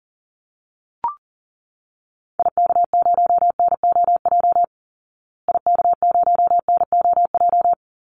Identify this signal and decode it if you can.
Morse code, heard as a keyed tone, about 40 wpm, 700 Hz. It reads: IK0NOJ IK0NOJ
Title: Call Signs Repeated 2x - 40wpm